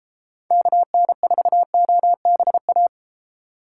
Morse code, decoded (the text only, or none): KN4OBA